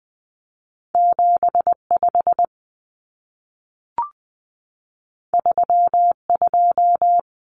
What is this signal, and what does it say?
Morse code, heard as a keyed tone, about 20 wpm, 700 Hz. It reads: 75 32
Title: Common Cut Numbers - 2-Digit 20wpm